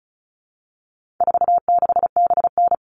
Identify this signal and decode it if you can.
Morse code, heard as a keyed tone, about 34 wpm, 700 Hz. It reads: 46BN
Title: All Cut Numbers - 4-Digit 35wpm